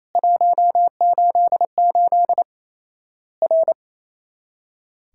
Morse code, heard as a keyed tone, about 28 wpm, 700 Hz. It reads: 188 R
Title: Worked All Europe DX Contest - QTC Exchange - 28wpm